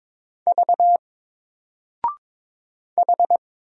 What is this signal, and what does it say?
Morse code, heard as a keyed tone, about 22 wpm, 700 Hz. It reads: V H
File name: Single-Letter-Number-22wpm